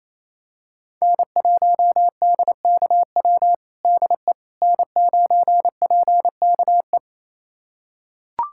Morse code, heard as a keyed tone, about 28 wpm, 700 Hz. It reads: N1DKW DE N9PKE
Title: USA Callsign DE Callsign - 5 Characters - 28wpm